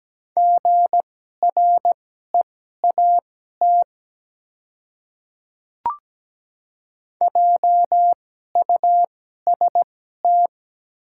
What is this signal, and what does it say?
Morse code, heard as a keyed tone, about 17 wpm, spaced as for 12 wpm, 700 Hz. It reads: GREAT JUST
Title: 2x Character Spacing - Top 300 Words - 17wpm